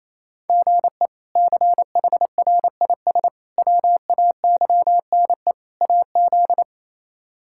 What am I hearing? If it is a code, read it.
Morse code, heard as a keyed tone, about 28 wpm, 700 Hz. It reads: GE CHRIS WAYNE AZ
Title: K1USN SST QSO Practice - 28wpm